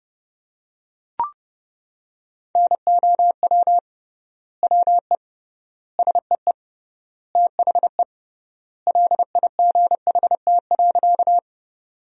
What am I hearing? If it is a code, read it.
Morse code, heard as a keyed tone, about 30 wpm, 700 Hz. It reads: NOW WE SEE THE LIGHT.
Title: 3x Word-Spacing - Sentences from Top 200 Words - 30wpm